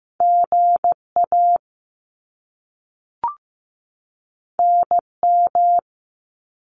Morse code, heard as a keyed tone, about 15 wpm, 700 Hz. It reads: GA NM
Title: US States 15wpm